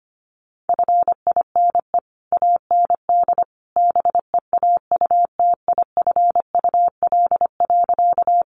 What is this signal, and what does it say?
Morse code, heard as a keyed tone, about 25 wpm, 700 Hz. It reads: FINE AND BEAUTIFUL.
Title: Sentences from Top 900 Words 25wpm